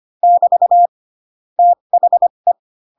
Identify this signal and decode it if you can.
Morse code, heard as a keyed tone, about 25 wpm, spaced as for 22 wpm, 700 Hz. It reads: <BT> THE
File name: Pinocchio / Pinocchio ch7 18 25 ik1hge